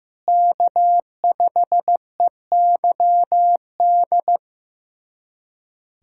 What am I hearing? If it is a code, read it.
Morse code, heard as a keyed tone, about 15 wpm, 700 Hz. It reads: K5EYD
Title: USA Callsign DE Callsign 15wpm